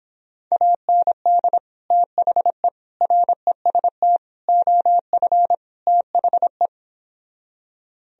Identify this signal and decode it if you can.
Morse code, heard as a keyed tone, about 26 wpm, 700 Hz. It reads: AND THE REST OF THE